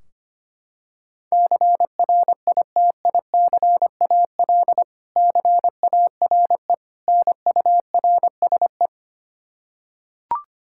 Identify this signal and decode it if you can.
Morse code, heard as a keyed tone, about 25 wpm, 700 Hz. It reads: CRITICAL CARE NURSE
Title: Practice QSO - Exchange 3 - More Info - 25wpm